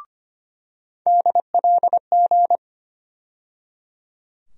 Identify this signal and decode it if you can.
Morse code, heard as a keyed tone, about 25 wpm, 700 Hz. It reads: DLG
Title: USA Call Sign Postfixes - 3 Letters 25wpm